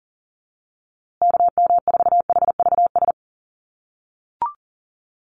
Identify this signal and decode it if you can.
Morse code, heard as a keyed tone, about 40 wpm, 700 Hz. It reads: KM4HVS
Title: USA Call Signs Repeated 2x 40wpm